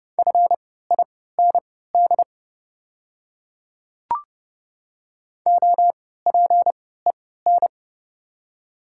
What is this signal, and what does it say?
Morse code, heard as a keyed tone, about 30 wpm, spaced as for 17 wpm, 690 Hz. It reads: FIND OPEN
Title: 3x Character Spacing - Top 300 Words - 30wpm